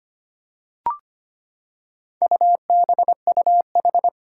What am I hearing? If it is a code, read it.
Morse code, heard as a keyed tone, about 25 wpm, 700 Hz. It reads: UBUH
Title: Four Challenging Letter-Number Combinations 25wpm